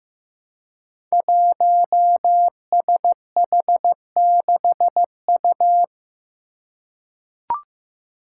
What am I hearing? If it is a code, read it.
Morse code, heard as a keyed tone, about 15 wpm, 700 Hz. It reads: 1SH6U
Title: Five Challenging Letter-Number Combinations 15wpm